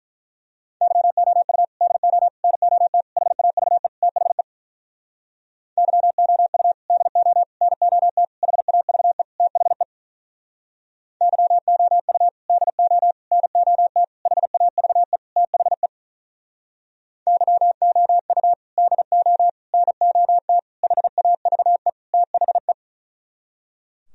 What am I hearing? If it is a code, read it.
Morse code, heard as a keyed tone, about 44 wpm, 700 Hz. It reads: YOU DO NOT HAVE THE YOU DO NOT HAVE THE YOU DO NOT HAVE THE YOU DO NOT HAVE THE